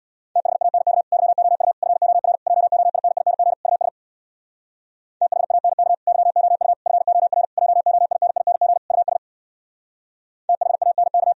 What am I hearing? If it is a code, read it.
Morse code, heard as a keyed tone, about 64 wpm, 700 Hz. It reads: THANK YOU FOR JOINING US THANK YOU FOR JOINING US THANK